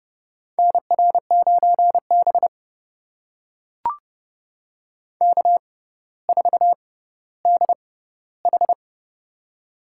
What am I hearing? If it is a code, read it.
Morse code, heard as a keyed tone, about 30 wpm, spaced as for 15 wpm, 700 Hz. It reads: NR9B K4DH
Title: USA Call Signs - 4 Characters - 6x Character Spacing - 30wpm